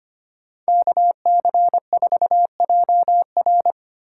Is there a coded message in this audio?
Yes: Morse code from 0.7 to 3.7 s, heard as a keyed tone, about 33 wpm, 700 Hz. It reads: KC4JR